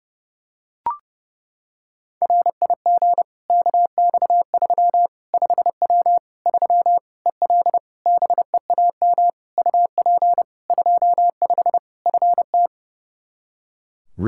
Morse code, heard as a keyed tone, about 30 wpm, 700 Hz. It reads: RIG KX3 5W 3 EL BEAM UP 25 FT
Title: K1USN SST QSO Practice 30wpm